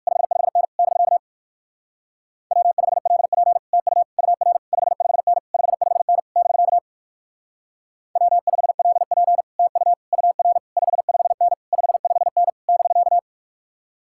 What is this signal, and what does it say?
Morse code, heard as a keyed tone, about 56 wpm, 700 Hz. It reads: 55N <BK> W5LP TU UR 55N 55N <BK> W5LP TU UR 55N 55N <BK>